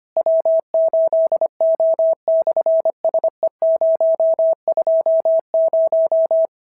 Morse code, heard as a keyed tone, about 24 wpm, 650 Hz. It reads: W8O/SE020
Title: SOTA QSO Practice 25wpm